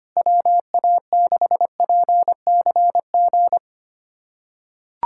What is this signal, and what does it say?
Morse code, heard as a keyed tone, about 25 wpm, 700 Hz. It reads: WA6PCG
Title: USA Callsign DE Callsign 25wpm